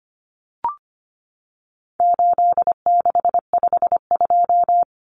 Morse code, heard as a keyed tone, about 25 wpm, 700 Hz. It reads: 8652
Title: Common Cut Numbers - 4-Digit 25wpm